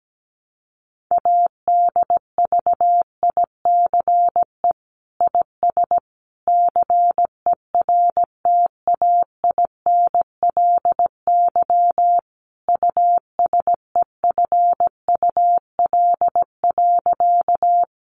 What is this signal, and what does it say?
Morse code, heard as a keyed tone, about 17 wpm, 700 Hz. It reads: ADVICE IS CERTAINLY USEFUL.
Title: Sentences from Top 1100 Words 17wpm